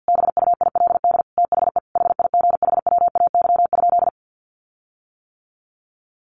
Morse code, heard as a keyed tone, about 50 wpm, 700 Hz. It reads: BUILD THE HIGHWAY?
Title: Sentences from Top 1400 Words 50wpm